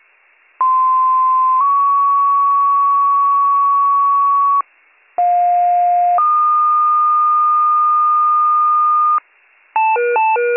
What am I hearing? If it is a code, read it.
Two-tone page, heard as a one-second tone, then a three-second tone, about 1040 then 1120 Hz, about 710 then 1220 Hz. It is heard from 0.6 to 9.2 s.